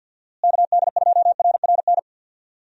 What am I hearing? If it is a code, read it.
Morse code, heard as a keyed tone, about 52 wpm, 700 Hz. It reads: KD2RRN
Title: USA Call Signs - 6 Characters - Repeated 3x - 50wpm